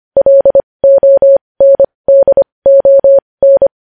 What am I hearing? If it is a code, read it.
Morse code, heard as a keyed tone, about 25 wpm, spaced as for 20 wpm, 550 Hz. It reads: LONDON
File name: podcast / News headlines-1x-25x20